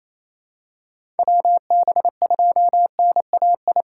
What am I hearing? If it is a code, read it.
Morse code, heard as a keyed tone, about 28 wpm, 700 Hz. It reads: WB2NAI
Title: USA Call Signs - 6 Characters - 3x Character Spacing - 28wpm